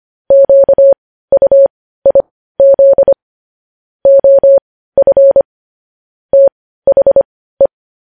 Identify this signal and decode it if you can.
Morse code, heard as a keyed tone, about 25 wpm, spaced as for 15 wpm, 550 Hz. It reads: QUIZ OF THE
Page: News Headlines in Morse Code at 15 WPM